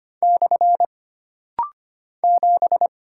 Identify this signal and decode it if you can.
Morse code, heard as a keyed tone, about 25 wpm, 700 Hz. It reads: / 7
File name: Single Character (Letter:Number:Punctuation:Prosign) - Rapid-Fire - 25wpm